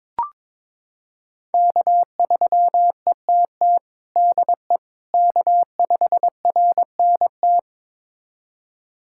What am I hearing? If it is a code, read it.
Morse code, heard as a keyed tone, about 21 wpm, 700 Hz. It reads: K3ETT DE K5RNT
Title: USA Callsign DE Callsign - 5 Characters - 22wpm